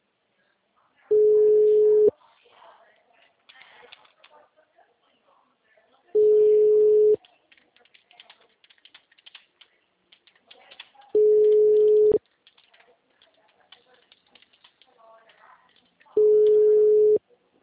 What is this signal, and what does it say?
Ringback tone (Europe, 425 Hz), with speech.